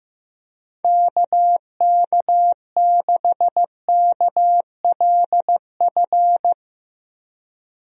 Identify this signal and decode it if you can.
Morse code, heard as a keyed tone, about 16 wpm, 700 Hz. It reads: KK6KLF
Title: USA Call Signs - 6 Characters - 15wpm